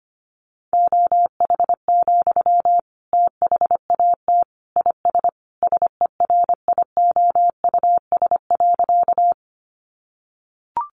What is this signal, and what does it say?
Morse code, heard as a keyed tone, about 25 wpm, 700 Hz. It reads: OH, THAT IS SERIOUS.
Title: Sentences from Top 1100 Words 25wpm